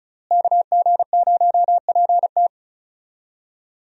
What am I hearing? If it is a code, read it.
Morse code, heard as a keyed tone, about 36 wpm, 700 Hz. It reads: KG0PT
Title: USA Call Signs - 5 Characters - Repeated 3x - 35wpm